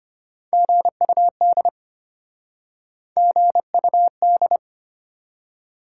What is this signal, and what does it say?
Morse code, heard as a keyed tone, about 33 wpm, 700 Hz. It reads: GUD GUD